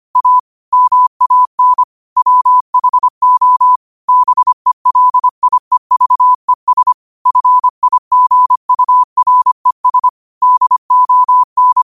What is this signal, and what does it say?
Morse code, heard as a keyed tone, about 25 wpm, 1000 Hz. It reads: A MAN WHO BELIEVES FIGURES DON
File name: qotdfiles / 25wpm